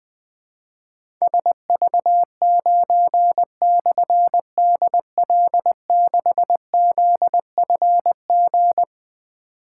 Morse code, heard as a keyed tone, about 20 wpm, 700 Hz. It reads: SV9/DL6ZFG
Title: Difficult Call Signs 20wpm